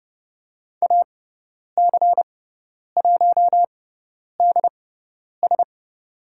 Morse code, heard as a keyed tone, about 30 wpm, spaced as for 10 wpm, 700 Hz. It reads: AC1DS
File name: Lesson 18g - Letter m - Call Signs 30wpm